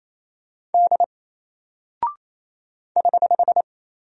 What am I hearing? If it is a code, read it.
Morse code, heard as a keyed tone, about 27 wpm, 700 Hz. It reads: D <HH>
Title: Single Character (Letter:Number:Punctuation:Prosign) 28wpm